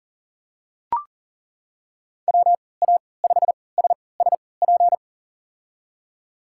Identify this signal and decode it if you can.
Morse code, heard as a keyed tone, about 37 wpm, spaced as for 22 wpm, 700 Hz. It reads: WA5SSP
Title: USA Call Signs - 3x Character Spacing - 40wpm